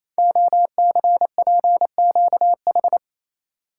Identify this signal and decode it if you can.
Morse code, heard as a keyed tone, about 29 wpm, 700 Hz. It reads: OCPQH
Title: Code Groups - 5 Letter:Numbers - 28wpm